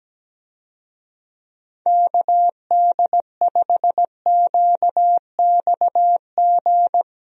Morse code, heard as a keyed tone, about 17 wpm, 700 Hz. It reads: KD5QXG
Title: USA Call Signs - 6 Characters - Repeated 3x - 17wpm